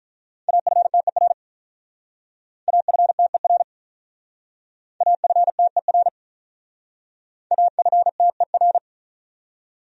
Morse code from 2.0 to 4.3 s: AFTER

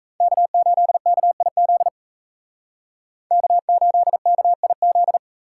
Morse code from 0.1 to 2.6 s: K8KIZ

